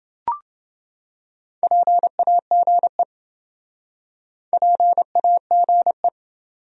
Morse code, heard as a keyed tone, about 28 wpm, 700 Hz. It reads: PAGE PAGE